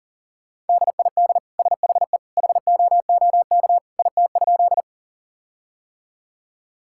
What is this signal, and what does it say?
Morse code, heard as a keyed tone, about 40 wpm, 700 Hz. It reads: DID SHE HOOK IT?
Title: Sentences from Top 1600 Words 40wpm